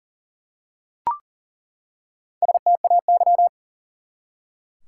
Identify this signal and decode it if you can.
Morse code, heard as a keyed tone, about 40 wpm, 700 Hz. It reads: STAY
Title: Four Letter Words - 40wpm